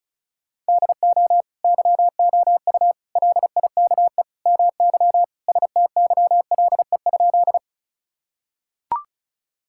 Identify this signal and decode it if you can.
Morse code, heard as a keyed tone, about 34 wpm, 700 Hz. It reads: DO YOU LIKE MY STYLE?
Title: Sentences from Top 1000 Words - Review Part 5 - 35wpm